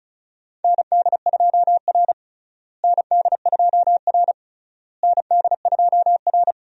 Morse code, heard as a keyed tone, about 34 wpm, 700 Hz. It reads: ND2R ND2R ND2R
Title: USA Call Signs - 4 Characters - Repeated 3x - 35wpm